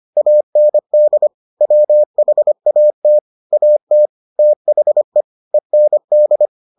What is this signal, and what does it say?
Morse code, heard as a keyed tone, about 25 wpm, 600 Hz. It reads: AND WHAT AT THE END